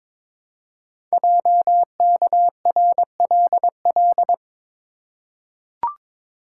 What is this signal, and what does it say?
Morse code, heard as a keyed tone, about 22 wpm, 700 Hz. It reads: JKRLL
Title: Code Groups - 5 Letters - 22wpm